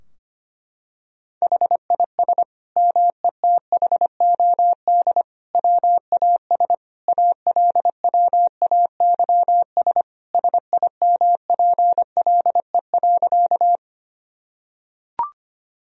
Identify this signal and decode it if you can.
Morse code, heard as a keyed tone, about 25 wpm, 700 Hz. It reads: HIS METHOD WAS ALWAYS SIMPLE.
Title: Sentences from Top 1000 Words - Review Part 3 - 25wpm